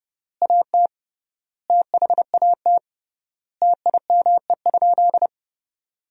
Morse code, heard as a keyed tone, about 31 wpm, 700 Hz. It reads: AT THAT TIME?
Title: 3x Word-Spacing - Sentences from Top 200 Words - 30wpm